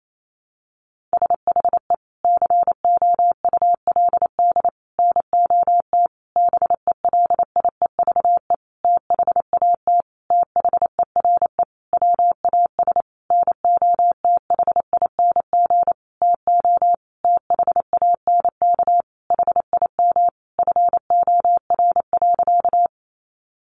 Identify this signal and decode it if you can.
Morse code, heard as a keyed tone, about 28 wpm, 700 Hz. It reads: SHE COULD NOT BELIEVE THAT THERE WAS NOTHING TO THANK HIM FOR.
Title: Sentences from Top 1000 Words 28wpm